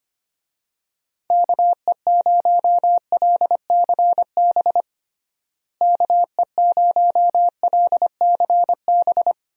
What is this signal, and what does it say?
Morse code, heard as a keyed tone, about 25 wpm, 700 Hz. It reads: KE0LCB KE0LCB
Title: USA Call Signs - 6 Characters - Repeated 3x - 25wpm